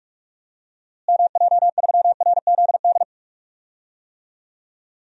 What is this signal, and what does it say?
Morse code, heard as a keyed tone, about 42 wpm, 700 Hz. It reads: MJ3RZD
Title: DXCC-100 Call Signs - 6 Characters - Repeated 3x - 45wpm